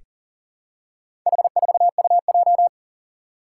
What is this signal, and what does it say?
Morse code, heard as a keyed tone, about 40 wpm, 700 Hz. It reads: H4UJ
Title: Four Challenging Letter-Number Combinations 40wpm